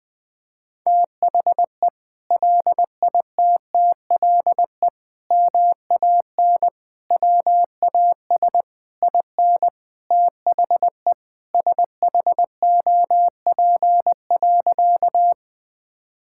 Morse code, heard as a keyed tone, about 20 wpm, 700 Hz. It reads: THE LITTLE MAN WAS IN THE SHOP.
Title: Sentences from Top 1600 Words 20wpm